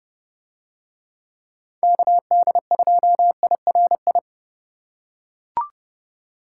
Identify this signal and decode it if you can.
Morse code, heard as a keyed tone, about 29 wpm, 700 Hz. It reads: KD2IRI